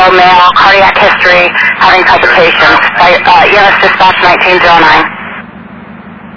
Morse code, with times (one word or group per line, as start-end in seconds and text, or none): none